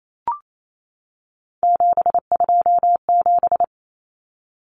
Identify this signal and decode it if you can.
Morse code, heard as a keyed tone, about 28 wpm, 700 Hz. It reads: 727
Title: Numbers - 3-Digit 28wpm